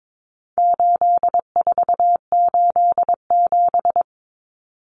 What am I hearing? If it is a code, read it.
Morse code, heard as a keyed tone, about 22 wpm, 700 Hz. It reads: 8487
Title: Common Cut Numbers - 4-Digit 22wpm